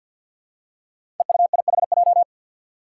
Morse code, heard as a keyed tone, about 47 wpm, 700 Hz. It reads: EUIHJ
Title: Five Challenging Letter-Number Combinations 50wpm